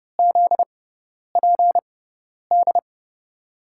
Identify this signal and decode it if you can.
Morse code, heard as a keyed tone, about 30 wpm, spaced as for 10 wpm, 700 Hz. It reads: ZPD
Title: USA Call Signs - 6 Characters - 6x Character Spacing - 30wpm